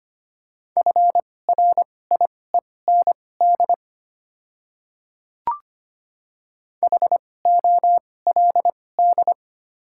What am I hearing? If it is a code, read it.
Morse code, heard as a keyed tone, about 25 wpm, spaced as for 18 wpm, 700 Hz. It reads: FRIEND HOLD
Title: 2x Character Spacing - Top 500 Words - 25wpm